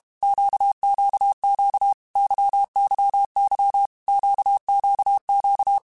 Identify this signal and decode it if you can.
Morse code, heard as a keyed tone, about 32 wpm, 800 Hz. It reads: QQQ YYY QQQ
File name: Groups@30 / Groups@30 mp3s / LFCYQ@30